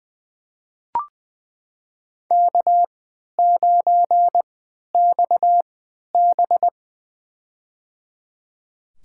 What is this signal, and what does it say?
Morse code, heard as a keyed tone, about 20 wpm, spaced as for 11 wpm, 700 Hz. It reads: K9XB
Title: USA Call Signs - 4 Characters - 3x Character Spacing - 20wpm